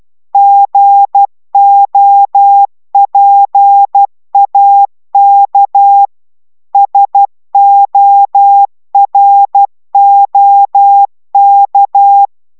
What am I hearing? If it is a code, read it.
Morse code, heard as a keyed tone, about 12 wpm, 800 Hz. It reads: GOPAK SOROK
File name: taviro tanulas / cw 69 15 10